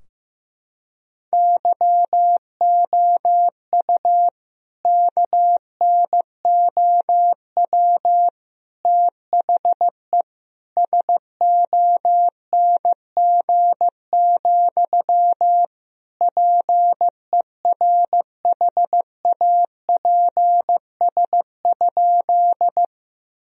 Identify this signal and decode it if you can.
Morse code, heard as a keyed tone, about 15 wpm, 700 Hz. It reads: YOU KNOW THE SONG, PERHAPS?
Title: Sentences from Top 1000 Words - Review Part 2 - 15wpm